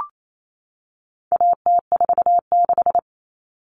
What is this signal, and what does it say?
Morse code, heard as a keyed tone, about 28 wpm, 700 Hz. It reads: AT46
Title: All Cut Numbers - 4-Digit 28wpm